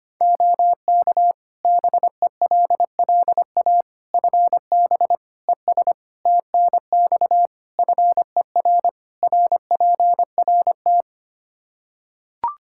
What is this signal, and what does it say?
Morse code, heard as a keyed tone, about 25 wpm, 700 Hz. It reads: OK BELLA FB ES TNX FER RPRT
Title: K1USN SST QSO Practice 25wpm